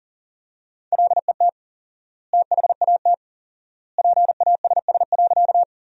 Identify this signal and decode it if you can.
Morse code, heard as a keyed tone, about 40 wpm, 700 Hz. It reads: LET THAT PASS.
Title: 4x Word-Spacing - Sentences from Top 400 Words - 40wpm